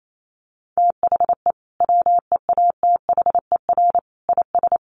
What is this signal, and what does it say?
Morse code, heard as a keyed tone, about 28 wpm, 700 Hz. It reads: THE WEATHER IS
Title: Sentences from Top 1200 Words 28wpm